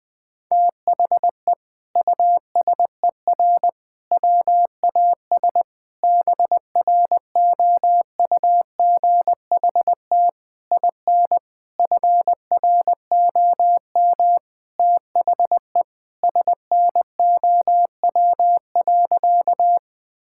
Morse code, heard as a keyed tone, about 20 wpm, 700 Hz. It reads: THE USER WAS BROUGHT IN FROM THE SNOW.
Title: Sentences from Top 1000 Words - Review Part 5 - 20wpm